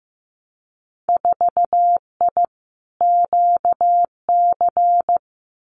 Morse code, heard as a keyed tone, about 15 wpm, 700 Hz. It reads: 4I QC
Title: Winter Field Day Exchanges Encore 15wpm